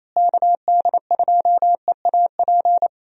Morse code, heard as a keyed tone, about 28 wpm, 700 Hz. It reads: KD2EAP
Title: USA Call Signs - 6 Characters - Repeated 2x - 28wpm